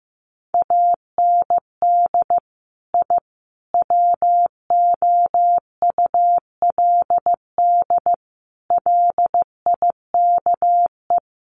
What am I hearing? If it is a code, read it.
Morse code, heard as a keyed tone, about 15 wpm, 700 Hz. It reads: AND I WOULD LIKE
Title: Sets of 5 Words Encore 15wpm